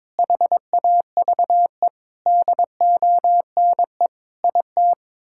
Morse code, heard as a keyed tone, about 22 wpm, 700 Hz. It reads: HAVE DONE IT
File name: Difficult Sentences with Hints - 22wpm